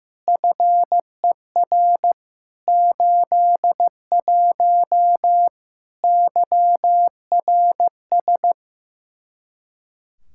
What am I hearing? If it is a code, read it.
Morse code, heard as a keyed tone, about 15 wpm, 700 Hz. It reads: FER 81 YRS
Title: Practice QSO - Exchange 3 - More Info - 15wpm